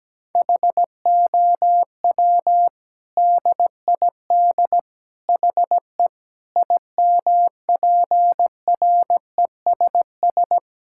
Morse code, heard as a keyed tone, about 17 wpm, 700 Hz. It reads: HOW DID HE IMPRESS
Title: Sentences from Top 1600 Words 17wpm